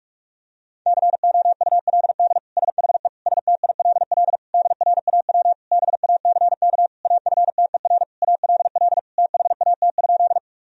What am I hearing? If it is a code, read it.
Morse code, heard as a keyed tone, about 41 wpm, 700 Hz. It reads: COULD SHE STILL DRAW BACK AFTER ALL THAT?
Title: Sentences from Top 1000 Words - Review Part 1 - 45wpm